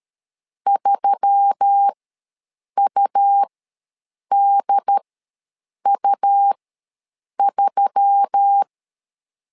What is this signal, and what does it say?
Morse code, heard as a keyed tone, about 13 wpm, spaced as for 7 wpm, 800 Hz. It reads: 3UDU3
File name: LEZIONI / 4 Q1S3 ws 12wpm 2038